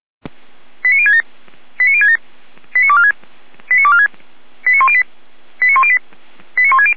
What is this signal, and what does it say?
Five-tone selective call (ZVEI), heard as quick runs of five tones, tones of about 70 ms. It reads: 89067 89067 89246 89246 89108 89108 89160
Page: audio.f85360.feuerwehr-orlamuende.org